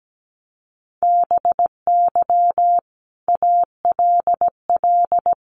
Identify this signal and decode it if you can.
Morse code, heard as a keyed tone, about 17 wpm, 700 Hz. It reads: BY ALL
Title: Sentences from Top 1300 Words 17wpm